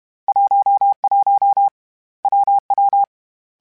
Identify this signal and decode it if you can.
Morse code, heard as a keyed tone, about 32 wpm, 800 Hz. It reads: 11 WW
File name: Difficult Combos / Difficult Combos-mp3s / WPJ WPJ1 @30